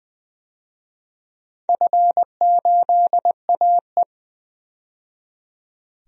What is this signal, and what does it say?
Morse code, heard as a keyed tone, about 20 wpm, 700 Hz. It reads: F8AE
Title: DXCC-100 Call Signs - 4 Characters - 20wpm